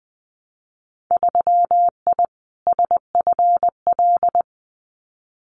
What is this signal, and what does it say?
Morse code, heard as a keyed tone, about 20 wpm, 700 Hz. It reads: 3I SFL
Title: Winter Field Day Exchanges 20wpm